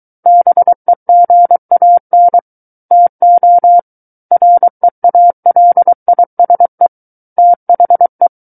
Morse code, heard as a keyed tone, about 23 wpm, 700 Hz. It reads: BEGAN TO REALISE THE
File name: Book One / 14-Chapter 14 - 23 WPM